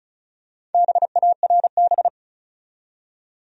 Morse code, heard as a keyed tone, about 35 wpm, 700 Hz. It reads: BARB
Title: USA Common Names - Max 5 Letters - 35wpm